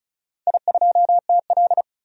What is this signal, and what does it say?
Morse code, heard as a keyed tone, about 36 wpm, 700 Hz. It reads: I2TL
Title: DXCC-100 Call Signs - 4 Characters - 35wpm